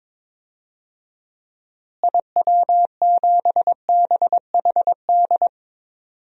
Morse code, heard as a keyed tone, about 22 wpm, 700 Hz. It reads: IW7BHD
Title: DXCC-100 Call Signs - 6 Characters - Repeated 3x - 22wpm